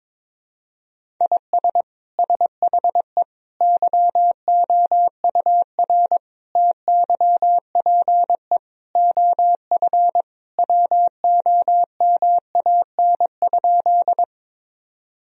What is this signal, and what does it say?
Morse code, heard as a keyed tone, about 22 wpm, 700 Hz. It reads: IS SHE YOUR TYPE OF WOMAN?
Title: Sentences from Top 1000 Words - Review Part 3 - 22wpm